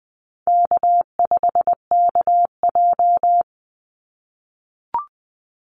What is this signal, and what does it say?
Morse code, heard as a keyed tone, about 20 wpm, 700 Hz. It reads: K5KJ
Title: Call Signs - 20wpm